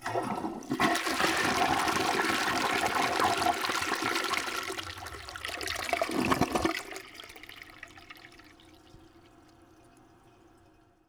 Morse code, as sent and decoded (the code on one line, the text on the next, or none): none